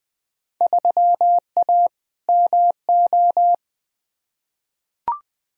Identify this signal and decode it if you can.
Morse code, heard as a keyed tone, about 20 wpm, 700 Hz. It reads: 3A MO
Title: Field Day Exchanges - 20wpm